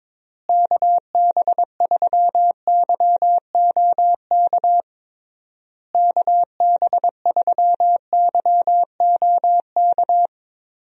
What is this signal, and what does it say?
Morse code, heard as a keyed tone, about 22 wpm, 700 Hz. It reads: KB3YOK KB3YOK
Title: USA Call Signs - 6 Characters - Repeated 3x - 22wpm